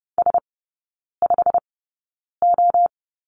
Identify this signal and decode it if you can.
Morse code, heard as a keyed tone, about 30 wpm, spaced as for 9 wpm, 700 Hz. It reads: S5O
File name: course / 30:10wpm / 087 Lesson 15e - Number 5 - Three Characters 30wpm